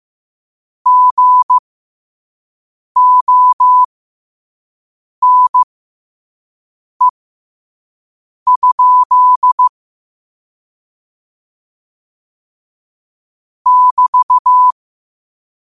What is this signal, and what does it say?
Morse code, heard as a keyed tone, about 15 wpm, spaced as for 5 wpm, 1000 Hz. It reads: GONE? <BT>
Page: New quotes every day in morse code at 5 Words per minute.